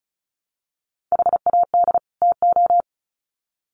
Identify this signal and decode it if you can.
Morse code, heard as a keyed tone, about 34 wpm, 700 Hz. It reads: HAD TO
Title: Sets of 2 Words 35wpm